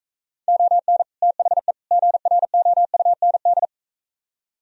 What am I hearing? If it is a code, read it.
Morse code, heard as a keyed tone, about 43 wpm, 700 Hz. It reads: ON THE GROUND